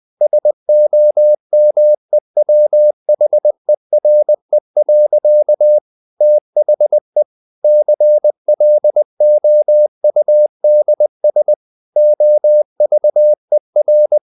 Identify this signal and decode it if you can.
Morse code, heard as a keyed tone, about 20 wpm, 600 Hz. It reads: SOMEWHERE. THE CLOUDS OVER
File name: the old man and the sea / the old man and the sea part 1 20wpm